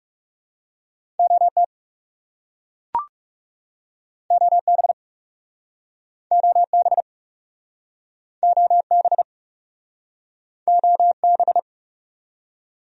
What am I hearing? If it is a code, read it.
Morse code, heard as a keyed tone, about 40 wpm, 700 Hz. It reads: OT OB OB OB OB